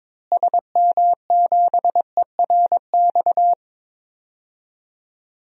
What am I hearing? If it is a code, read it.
Morse code, heard as a keyed tone, about 22 wpm, 700 Hz. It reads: SM7ERX
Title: DXCC-100 Call Signs - 6 Characters - Repeated 3x - 22wpm